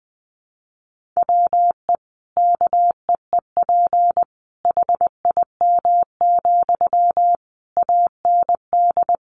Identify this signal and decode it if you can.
Morse code, heard as a keyed tone, about 20 wpm, 700 Hz. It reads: WE KEEP HIM, AND
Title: Sentences from Top 600 Words 20wpm